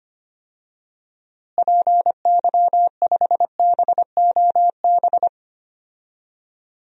Morse code, heard as a keyed tone, about 25 wpm, 700 Hz. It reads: PY5BOB
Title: DXCC-100 Call Signs - 6 Characters - Repeated 2x - 25wpm